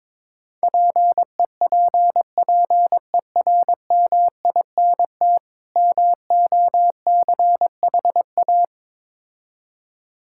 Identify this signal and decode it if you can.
Morse code, heard as a keyed tone, about 22 wpm, 700 Hz. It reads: PEPPERMINT MOCHA
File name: Difficult Words with Hints - 22wpm